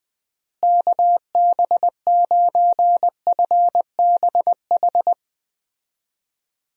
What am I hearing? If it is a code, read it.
Morse code, heard as a keyed tone, about 20 wpm, 700 Hz. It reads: KB9FBH
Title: USA Call Signs - 6 Characters - Repeated 2x - 20wpm